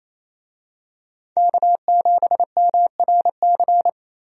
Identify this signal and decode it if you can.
Morse code, heard as a keyed tone, about 28 wpm, 700 Hz. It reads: K7MRC
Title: USA Call Signs - 5 Characters - Repeated 2x - 28wpm